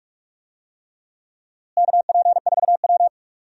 Morse code, heard as a keyed tone, about 46 wpm, 700 Hz. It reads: KP4W
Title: DXCC-100 Call Signs - 4 Characters - Repeated 3x - 45wpm